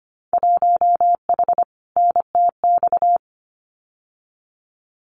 Morse code, heard as a keyed tone, about 25 wpm, 700 Hz. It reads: 1H NTX
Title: Winter Field Day Exchanges 25wpm